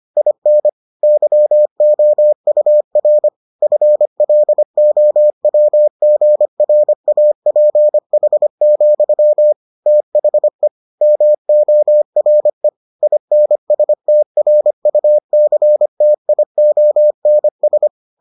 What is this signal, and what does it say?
Morse code, heard as a keyed tone, about 25 wpm, 600 Hz. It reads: IN YOUR FLOWGRAPH, THE MORE INSTRUCTIONS